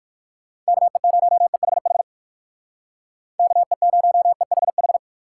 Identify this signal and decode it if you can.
Morse code, heard as a keyed tone, about 64 wpm, spaced as for 56 wpm, 700 Hz. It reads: KE0EHH KE0EHH
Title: Speed Racing - USA 2x3 Call Signs 45wpm